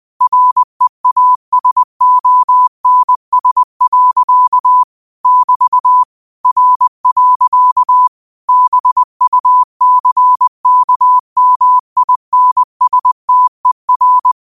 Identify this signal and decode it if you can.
Morse code, heard as a keyed tone, about 20 wpm, 1000 Hz. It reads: REASONS. <BT> R. BUCKMINSTER
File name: qotdfiles / 20wpm